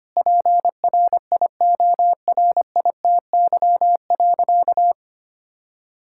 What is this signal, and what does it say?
Morse code, heard as a keyed tone, about 25 wpm, 700 Hz. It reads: PRIORITY.
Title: Sentences from Top 1600 Words 25wpm